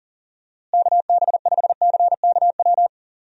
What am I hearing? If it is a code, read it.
Morse code, heard as a keyed tone, about 40 wpm, 700 Hz. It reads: KB5CKW
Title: USA Call Signs - 6 Characters - Repeated 2x - 40wpm